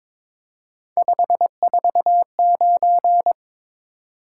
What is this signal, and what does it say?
Morse code, heard as a keyed tone, about 22 wpm, 700 Hz. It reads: 549
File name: practice / 22wpm / Three Letter-Number Combinations - 22wpm